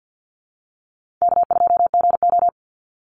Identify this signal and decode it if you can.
Morse code, heard as a keyed tone, about 50 wpm, 700 Hz. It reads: K2GO
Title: Easy Call Signs Repeated 2x - 50wpm